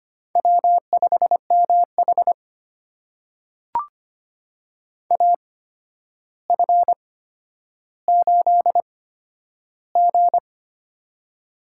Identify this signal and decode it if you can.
Morse code, heard as a keyed tone, about 25 wpm, spaced as for 11 wpm, 700 Hz. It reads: W5MH AF8G
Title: USA Call Signs - 4 Characters - 8x Character Spacing - 25wpm